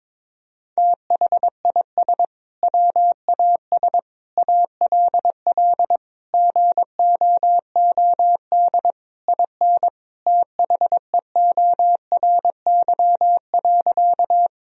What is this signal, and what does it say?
Morse code, heard as a keyed tone, about 22 wpm, 700 Hz. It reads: THIS WAS ALL GOOD IN THEORY.
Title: Sentences from Top 1000 Words - Review Part 3 - 22wpm